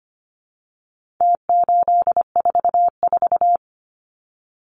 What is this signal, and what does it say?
Morse code, heard as a keyed tone, about 25 wpm, 700 Hz. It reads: T844
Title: Numbers - Contest Style - 4-Digit 25wpm